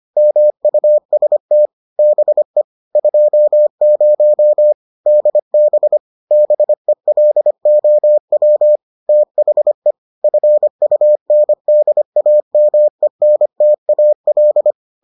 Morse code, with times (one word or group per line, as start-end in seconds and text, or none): 0.2-1.7 MUST
2.0-2.6 BE
3.0-4.7 20
5.1-6.0 DB
6.3-8.8 BELOW
9.1-9.9 THE
10.2-14.7 FUNDAMENTAL